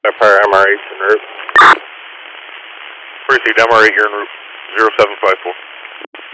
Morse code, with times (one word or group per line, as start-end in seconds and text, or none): none